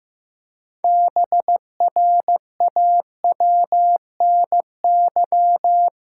Morse code, heard as a keyed tone, about 15 wpm, 700 Hz. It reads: BRAWNY
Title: Pangram Challenge v2 - 15wpm